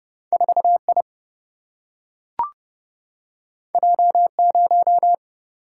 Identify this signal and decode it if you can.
Morse code, heard as a keyed tone, about 30 wpm, 700 Hz. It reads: 4 I J 0
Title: Single-Letter Single-Number 30wpm